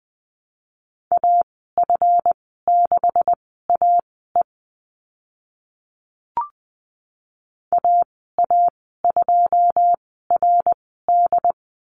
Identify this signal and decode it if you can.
Morse code, heard as a keyed tone, about 20 wpm, spaced as for 15 wpm, 700 Hz. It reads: AF6AE AA2RD
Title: 140 Lesson 23g - Letter f - Call Signs 20wpm